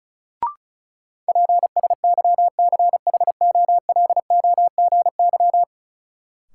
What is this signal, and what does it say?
Morse code, heard as a keyed tone, about 34 wpm, 700 Hz. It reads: PSYCHOLOGY
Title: Sentences from Top 1400 Words - New Word Drill Alt - 35wpm